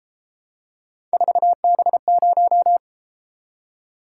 Morse code, heard as a keyed tone, about 31 wpm, 700 Hz. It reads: 4B0